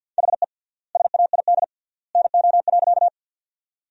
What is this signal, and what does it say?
Morse code, heard as a keyed tone, about 50 wpm, 700 Hz. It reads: HE SAID NO.
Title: 3x Word-Spacing - Sentences from Top 100 Words - 50wpm